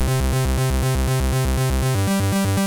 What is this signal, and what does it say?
There is no code.